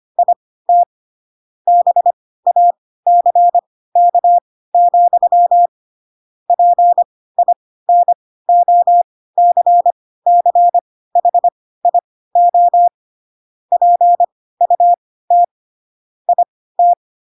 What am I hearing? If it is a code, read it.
Morse code, heard as a keyed tone, about 25 wpm, spaced as for 16 wpm, 700 Hz. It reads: IT BACK, PINOCCHIO PUT IT
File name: Pinocchio / Pinocchio ch3 10 25 ik1hge